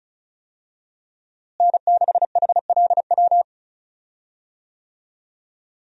Morse code, heard as a keyed tone, about 36 wpm, 700 Hz. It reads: N6HLW
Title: USA Call Signs - 5 Characters - Repeated 3x - 35wpm